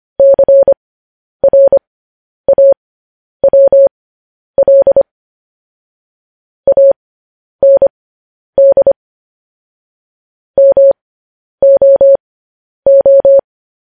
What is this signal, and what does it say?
Morse code, heard as a keyed tone, about 25 wpm, spaced as for 10 wpm, 550 Hz. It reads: CRAWL AND MOO